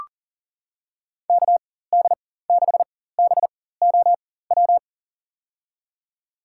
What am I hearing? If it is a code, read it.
Morse code, heard as a keyed tone, about 40 wpm, spaced as for 19 wpm, 700 Hz. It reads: KD6BOW
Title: USA Call Signs - 6 Characters - 4x Character Spacing - 40wpm